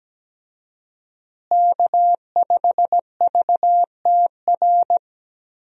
Morse code, heard as a keyed tone, about 17 wpm, 700 Hz. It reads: K5VTR
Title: USA Call Signs - 5 Characters - Repeated 2x - 17wpm